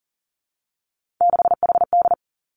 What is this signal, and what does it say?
Morse code, heard as a keyed tone, about 40 wpm, 700 Hz. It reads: 6HD
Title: Three Challenging Letter-Number Combinations 40wpm